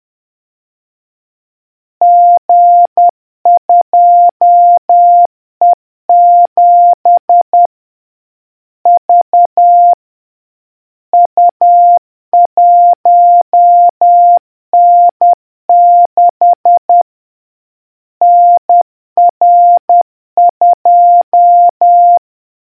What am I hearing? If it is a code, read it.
Morse code, heard as a keyed tone, about 10 wpm, 700 Hz. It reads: G2E7 V U1N6 NR2